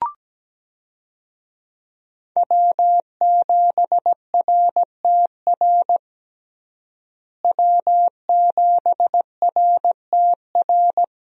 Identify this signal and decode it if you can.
Morse code, heard as a keyed tone, about 17 wpm, 700 Hz. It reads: W7RTR W7RTR
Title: USA Call Signs - 5 Characters - Repeated 3x - 17wpm